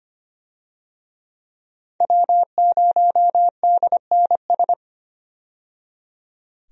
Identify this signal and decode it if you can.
Morse code, heard as a keyed tone, about 25 wpm, 700 Hz. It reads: W0DNS
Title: USA Call Signs - 5 Characters - 25wpm